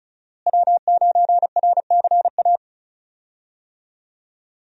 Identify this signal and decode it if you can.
Morse code, heard as a keyed tone, about 34 wpm, 700 Hz. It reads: W9RCA
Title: USA Call Signs - 5 Characters - Repeated 2x - 35wpm